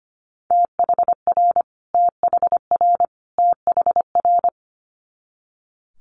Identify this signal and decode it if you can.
Morse code, heard as a keyed tone, about 25 wpm, 700 Hz. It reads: THR THR THR
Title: Multi-letter Phonemes Repeated 3x 25wpm